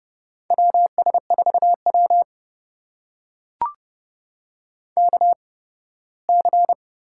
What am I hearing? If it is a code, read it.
Morse code, heard as a keyed tone, about 29 wpm, spaced as for 18 wpm, 700 Hz. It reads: WS4W KC
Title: USA Call Signs - 8x Character Spacing - 30wpm